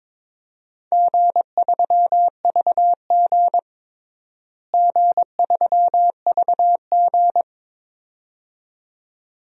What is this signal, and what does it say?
Morse code, heard as a keyed tone, about 22 wpm, 700 Hz. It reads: G3VG G3VG
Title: DXCC-100 Call Signs - 4 Characters - Repeated 2x - 22wpm